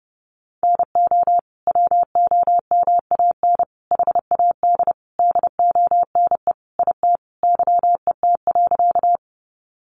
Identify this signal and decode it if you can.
Morse code, heard as a keyed tone, about 30 wpm, 700 Hz. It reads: NO WOMAN HAD DONE IT YET.
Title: Sentences from Top 1100 Words 30wpm